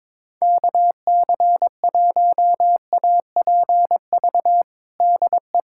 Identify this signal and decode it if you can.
Morse code, heard as a keyed tone, about 22 wpm, 700 Hz. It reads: KC1APV DE
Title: USA Callsign DE Callsign 22wpm